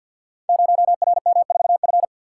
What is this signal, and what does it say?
Morse code, heard as a keyed tone, about 47 wpm, 700 Hz. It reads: 0RM4F
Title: Code Groups - 5 Letter:Numbers - 50wpm